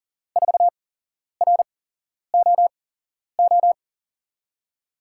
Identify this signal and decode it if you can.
Morse code, heard as a keyed tone, about 38 wpm, spaced as for 11 wpm, 700 Hz. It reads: 4ROO
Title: USA Call Signs - 6 Characters - 8x Character Spacing - 40wpm